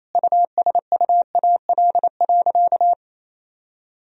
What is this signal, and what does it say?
Morse code, heard as a keyed tone, about 28 wpm, 700 Hz. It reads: USUAL.
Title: Sentences from Top 1000 Words - Review Part 2 - 28wpm